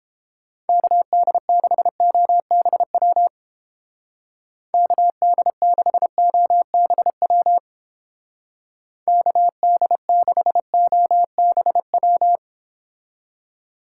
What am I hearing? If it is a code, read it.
Morse code, heard as a keyed tone, about 29 wpm, 700 Hz. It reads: KD6OBW KD6OBW KD6OBW